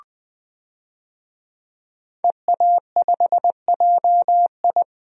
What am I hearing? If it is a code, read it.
Morse code, heard as a keyed tone, about 21 wpm, 700 Hz. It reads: EA5JI
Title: DXCC-100 Call Signs - 5 Characters - Repeated 3x - 20wpm